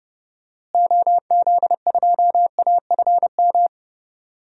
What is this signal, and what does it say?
Morse code, heard as a keyed tone, about 32 wpm, 700 Hz. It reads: OZ2AFM
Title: DXCC-100 Call Signs - 6 Characters - Repeated 2x - 30wpm